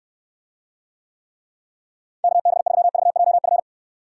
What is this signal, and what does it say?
Morse code, heard as a keyed tone, about 67 wpm, 700 Hz. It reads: KB3VJF